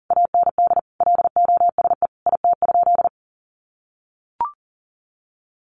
Morse code, heard as a keyed tone, about 38 wpm, 700 Hz. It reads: AND LOSE IT?
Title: Sentences from Top 1100 Words 40wpm